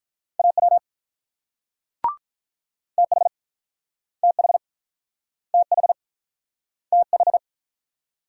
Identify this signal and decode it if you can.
Morse code, heard as a keyed tone, about 46 wpm, 700 Hz. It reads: AW TH TH TH TH